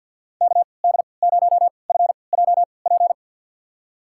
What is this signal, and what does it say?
Morse code, heard as a keyed tone, about 51 wpm, spaced as for 28 wpm, 700 Hz. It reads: KD0FJP
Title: USA Call Signs - 6 Characters - 3x Character Spacing - 50wpm